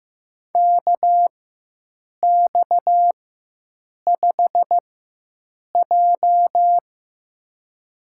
Morse code, heard as a keyed tone, about 16 wpm, spaced as for 7 wpm, 700 Hz. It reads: KX5J
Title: USA Call Signs - 4x Character Spacing - 15wpm